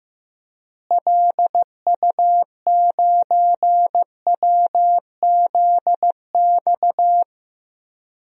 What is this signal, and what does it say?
Morse code, heard as a keyed tone, about 15 wpm, 700 Hz. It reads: LU9WZX
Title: DXCC-100 Call Signs - 6 Characters - Repeated 3x - 15wpm